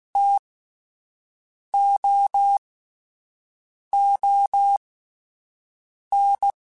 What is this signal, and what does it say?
Morse code, heard as a keyed tone, about 16 wpm, spaced as for 5 wpm, 800 Hz. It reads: TOON